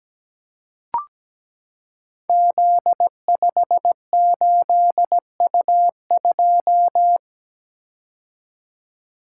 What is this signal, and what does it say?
Morse code, heard as a keyed tone, about 17 wpm, 700 Hz. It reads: Z58U2
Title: Code Groups - 5 Letter:Numbers - 17wpm